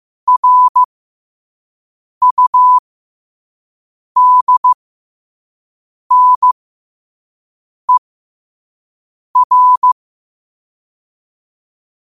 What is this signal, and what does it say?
Morse code, heard as a keyed tone, about 15 wpm, spaced as for 5 wpm, 1000 Hz. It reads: RUDNER
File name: qotdfiles / 5wpm